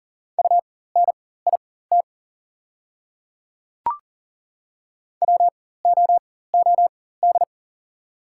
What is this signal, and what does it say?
Morse code, heard as a keyed tone, about 40 wpm, spaced as for 19 wpm, 690 Hz. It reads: UNIT WOOD
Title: 4x Character Spacing - Top 500 Words - 40wpm